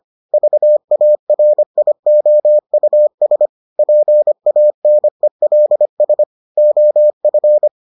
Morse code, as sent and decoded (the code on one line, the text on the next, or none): ...- .- .-. .. --- ..- ... / .--. .- -. .-.. ... / --- ..-.
VARIOUS PANELS OF